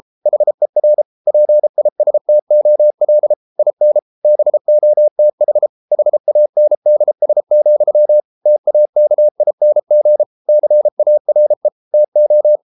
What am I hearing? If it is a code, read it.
Morse code, heard as a keyed tone, about 33 wpm, 600 Hz. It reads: HER PISTOL IN BOTH HANDS, TAKING CARE TO